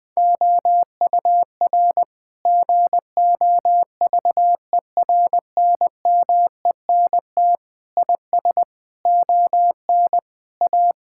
Morse code, heard as a keyed tone, about 21 wpm, 700 Hz. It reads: OUR GOVERNMENT IS ON A
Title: Sentences from Top 1000 Words - Review Part 5 - 20wpm